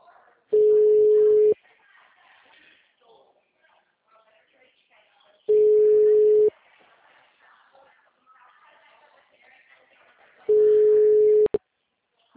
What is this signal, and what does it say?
Ringback tone (Europe, 425 Hz), with speech.